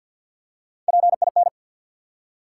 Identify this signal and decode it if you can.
Morse code, heard as a keyed tone, about 53 wpm, 700 Hz. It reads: PIN
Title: Three Letter Words - 50wpm